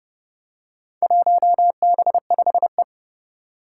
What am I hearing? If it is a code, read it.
Morse code, heard as a keyed tone, about 30 wpm, 700 Hz. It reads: 1B5E
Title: Four Challenging Letter-Number Combinations 30wpm